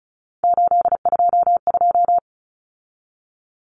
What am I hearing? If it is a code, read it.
Morse code, heard as a keyed tone, about 34 wpm, 700 Hz. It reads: 822
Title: Numbers - 3-Digit 35wpm